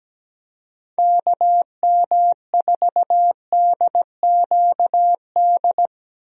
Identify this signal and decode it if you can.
Morse code, heard as a keyed tone, about 17 wpm, 700 Hz. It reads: KM4DQD
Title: USA Call Signs - 6 Characters - Repeated 3x - 17wpm